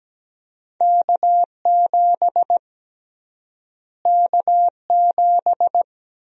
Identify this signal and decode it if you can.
Morse code, heard as a keyed tone, about 17 wpm, 700 Hz. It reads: K7 K7
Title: USA Call Sign Prefixes - Repeated 2x - 17wpm